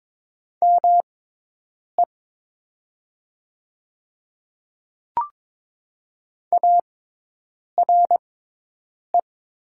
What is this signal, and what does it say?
Morse code, heard as a keyed tone, about 22 wpm, spaced as for 8 wpm, 700 Hz. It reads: ME ARE
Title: 6x Character Spacing - Top 500 Words - 22wpm